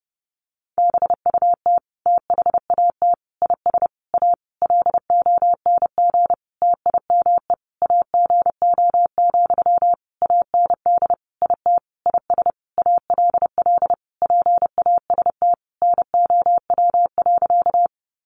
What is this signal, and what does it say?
Morse code, heard as a keyed tone, about 30 wpm, 700 Hz. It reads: BUT THAT IS A LONG TIME AGO, AND IT IS ALL PAST NOW.
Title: Sentences from Top 600 Words 30wpm